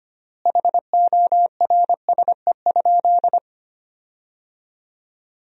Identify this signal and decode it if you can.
Morse code, heard as a keyed tone, about 25 wpm, 700 Hz. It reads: HORSE?
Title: 4x Word-Spacing - Sentences from Top 500 Words - 25wpm